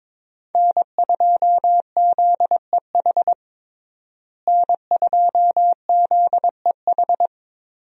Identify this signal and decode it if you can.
Morse code, heard as a keyed tone, about 22 wpm, 700 Hz. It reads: N2ZEH N2ZEH
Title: USA Call Signs - 5 Characters - Repeated 2x - 22wpm